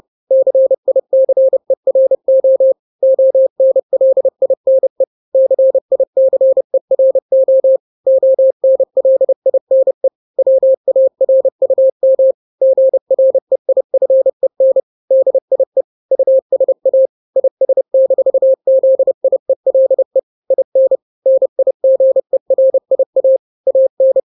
Morse code, heard as a keyed tone, about 30 wpm, 530 Hz. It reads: CICERO ONLINE CICERO ONLINE WARUM GREIFEN DIE USA IS-ZIELE IN NIGERIA AN
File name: daily / 29wpm